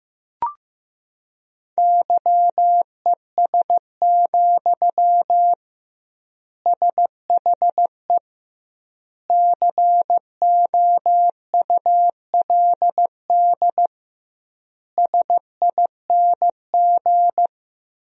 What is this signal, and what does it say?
Morse code, heard as a keyed tone, about 15 wpm, 700 Hz. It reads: YES, SHE COULD SING
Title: 2x Word-Spacing - Sentences from Top 500 Words - 15wpm